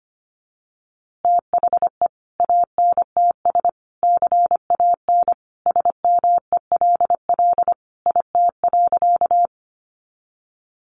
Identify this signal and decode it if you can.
Morse code, heard as a keyed tone, about 25 wpm, 700 Hz. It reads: THE ANTS CAN SMELL IT.
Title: Easy Phrases 25wpm